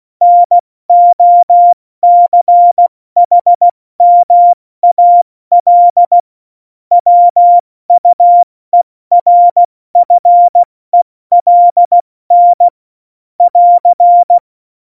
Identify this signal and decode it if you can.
Morse code, heard as a keyed tone, about 16 wpm, 700 Hz. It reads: NOCHMAL WUERFELN <AR>
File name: Morsekurs / Witze / Witz 22. Dezember 2020, 70 F80